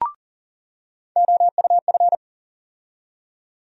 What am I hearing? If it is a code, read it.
Morse code, heard as a keyed tone, about 40 wpm, 700 Hz. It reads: OUF
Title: USA Call Sign Postfixes - 3 Letters 40wpm